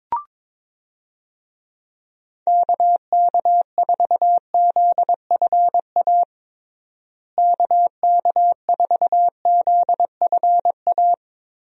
Morse code, heard as a keyed tone, about 21 wpm, 700 Hz. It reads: KK4ZFA KK4ZFA
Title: USA Call Signs - 6 Characters - Repeated 3x - 22wpm